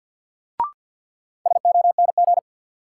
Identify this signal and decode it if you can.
Morse code, heard as a keyed tone, about 53 wpm, 700 Hz. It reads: SONG
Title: Sentences from Top 400 Words - New Word Drill - 50wpm